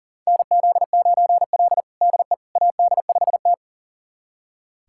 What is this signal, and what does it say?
Morse code, heard as a keyed tone, about 40 wpm, 700 Hz. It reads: NZ9L DE AD5T
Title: USA Callsign DE Callsign - 4 Characters - 40wpm